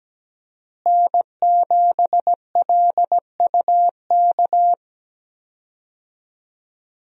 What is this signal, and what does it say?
Morse code, heard as a keyed tone, about 17 wpm, 700 Hz. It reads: N7LUK
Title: USA Call Signs - 5 Characters - Repeated 3x - 17wpm